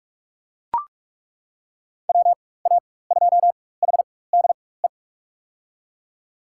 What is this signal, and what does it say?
Morse code, heard as a keyed tone, about 41 wpm, spaced as for 20 wpm, 700 Hz. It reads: WA2HDE
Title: USA Call Signs - 6 Characters - 4x Character Spacing - 45wpm